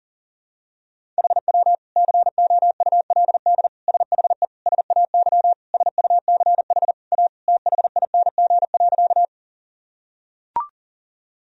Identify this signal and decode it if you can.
Morse code, heard as a keyed tone, about 40 wpm, 700 Hz. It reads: HW COULD SHE SAY SUCH A THING.
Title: Sentences from Top 200 Words and Abbreviations 40wpm